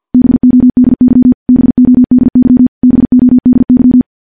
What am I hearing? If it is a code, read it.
Morse code, heard as a keyed tone, about 50 wpm, 250 Hz. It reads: BODY BODY BODY
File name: morse / morse-training-50WPM